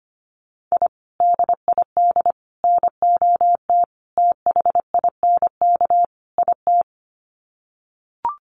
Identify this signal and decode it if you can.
Morse code, heard as a keyed tone, about 25 wpm, 700 Hz. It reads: I DID NOT THINK IT
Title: Sets of 5 Words Encore2 25wpm